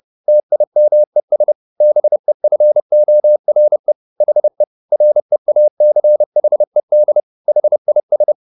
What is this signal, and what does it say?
Morse code, heard as a keyed tone, about 30 wpm, 600 Hz. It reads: TIMES BEFORE HE REACHED HIS